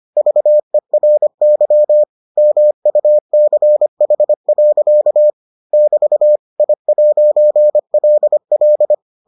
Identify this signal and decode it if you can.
Morse code, heard as a keyed tone, about 25 wpm, 600 Hz. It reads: VERY MUCH. <BT> I'LL